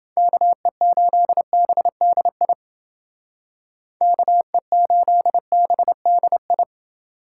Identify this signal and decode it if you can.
Morse code, heard as a keyed tone, about 29 wpm, 700 Hz. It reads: KE8BDI KE8BDI